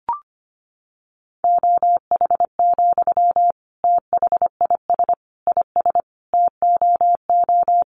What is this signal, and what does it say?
Morse code, heard as a keyed tone, about 25 wpm, 700 Hz. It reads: OH, THIS IS TOO
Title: Sentences from Top 500 Words 25wpm